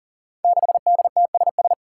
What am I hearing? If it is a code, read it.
Morse code, heard as a keyed tone, about 42 wpm, 700 Hz. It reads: 6DTSS
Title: Five Challenging Letter-Number Combinations 40wpm